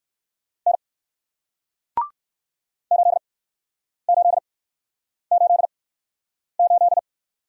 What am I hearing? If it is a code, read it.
Morse code, heard as a keyed tone, about 60 wpm, spaced as for 9 wpm, 700 Hz. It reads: N 8888